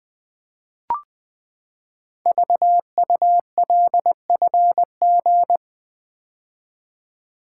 Morse code, heard as a keyed tone, about 20 wpm, 700 Hz. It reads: VULFG
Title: Code Groups - 5 Letters - 20wpm